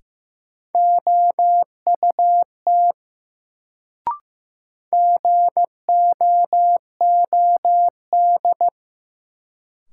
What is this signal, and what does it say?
Morse code, heard as a keyed tone, about 15 wpm, 700 Hz. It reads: OUT GOOD
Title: Sets of 2 Words - New Word Drill - 15wpm